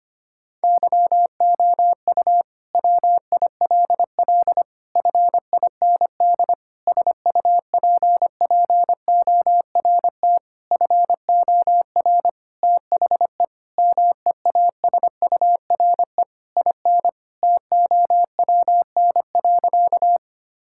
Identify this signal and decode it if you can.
Morse code, heard as a keyed tone, about 25 wpm, 700 Hz. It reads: YOU WILL FIND SUPPORT FOR THE MEASURE IN TOWN.
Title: Sentences from Top 1000 Words - Review Part 5 - 25wpm